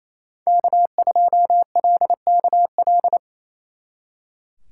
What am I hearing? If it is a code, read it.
Morse code, heard as a keyed tone, about 36 wpm, 700 Hz. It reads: K2LKL